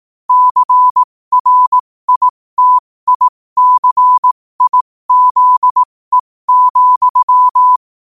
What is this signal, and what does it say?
Morse code, heard as a keyed tone, about 18 wpm, spaced as for 15 wpm, 1000 Hz. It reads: CRITICIZE,
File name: qotdfiles / 15wpm